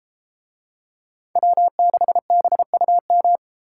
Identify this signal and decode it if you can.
Morse code, heard as a keyed tone, about 31 wpm, 700 Hz. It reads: W6BUM